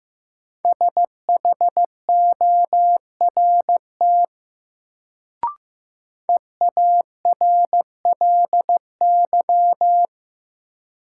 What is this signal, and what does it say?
Morse code, heard as a keyed tone, about 16 wpm, 700 Hz. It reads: SHORT EARLY
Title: Sentences from Top 400 Words - New Word Drill - 15wpm